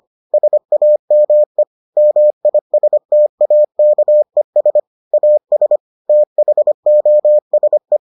Morse code, heard as a keyed tone, about 25 wpm, 600 Hz. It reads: SAME MISTAKES AS THOSE